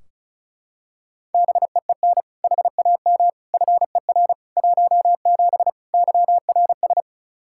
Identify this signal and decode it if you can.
Morse code, heard as a keyed tone, about 36 wpm, 700 Hz. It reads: BEEN HAM FER 17 YRS
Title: Practice QSO - Exchange 3 - More Info - 35wpm